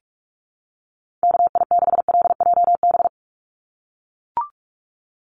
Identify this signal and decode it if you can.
Morse code, heard as a keyed tone, about 44 wpm, 700 Hz. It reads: KI6LJB
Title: USA Call Signs 45wpm